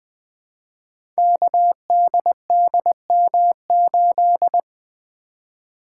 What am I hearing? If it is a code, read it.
Morse code, heard as a keyed tone, about 20 wpm, 700 Hz. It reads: KDDM8
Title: Code Groups - 5 Letter:Numbers - 20wpm